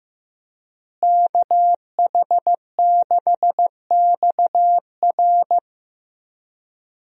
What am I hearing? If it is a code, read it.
Morse code, heard as a keyed tone, about 15 wpm, 700 Hz. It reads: KH6XR
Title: DXCC-100 Call Signs - 5 Characters - Repeated 3x - 15wpm